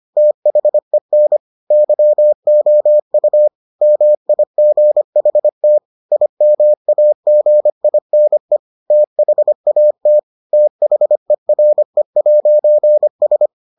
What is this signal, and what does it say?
Morse code, heard as a keyed tone, about 25 wpm, 600 Hz. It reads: THEN YOU MIGHT IMAGINE THAT THERE'S